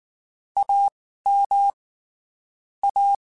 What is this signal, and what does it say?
Morse code, heard as a keyed tone, about 19 wpm, spaced as for 14 wpm, 800 Hz. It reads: AM A
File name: (spaced) / QSOs Spaced-mp3s / kc7haa @18 sp